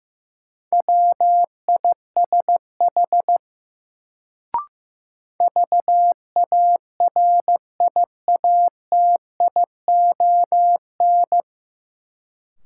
Morse code, heard as a keyed tone, about 15 wpm, 700 Hz. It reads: WISH VARIATION
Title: Sentences from Top 1600 Words - New Word Drill - 15wpm